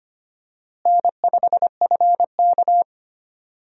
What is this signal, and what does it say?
Morse code, heard as a keyed tone, about 25 wpm, 700 Hz. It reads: N5FK
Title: USA Call Signs - 4 Characters - Repeated 2x - 25wpm